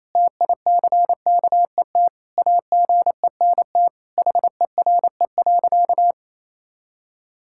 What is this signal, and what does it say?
Morse code, heard as a keyed tone, about 28 wpm, 700 Hz. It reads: TICKET AGENT HERE.
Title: Sentences from Top 1600 Words 28wpm